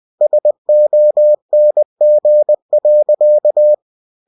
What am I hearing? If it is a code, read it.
Morse code, heard as a keyed tone, about 20 wpm, 600 Hz. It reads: SONG.
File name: cw-books / winnie-the-pooh / winnie-20wpm-page-0003